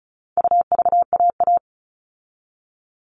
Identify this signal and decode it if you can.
Morse code, heard as a keyed tone, about 34 wpm, 700 Hz. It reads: UVAA
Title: All Cut Numbers - 4-Digit 35wpm